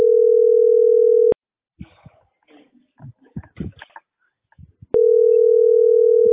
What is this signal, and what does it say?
Ringback tone (North America), 440 + 480 Hz.